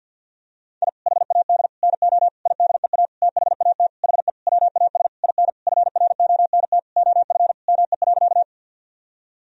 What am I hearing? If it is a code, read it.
Morse code, heard as a keyed tone, about 52 wpm, 700 Hz. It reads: I HAD NO IDEA THAT HE WAS IN FRONT OF ME.
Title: Sentences from Top 1000 Words - Review Part 2 - 50wpm